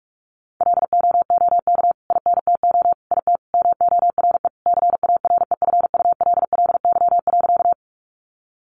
Morse code, heard as a keyed tone, about 44 wpm, 700 Hz. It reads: LOOK INTO IT MORE CAREFULLY.
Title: Sentences from Top 1300 Words 45wpm